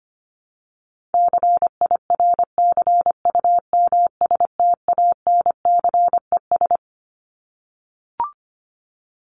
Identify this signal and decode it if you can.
Morse code, heard as a keyed tone, about 25 wpm, 700 Hz. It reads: CIRCUMSTANCES
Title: ES:S Suffix - Top 100 Words with Base Words 25wpm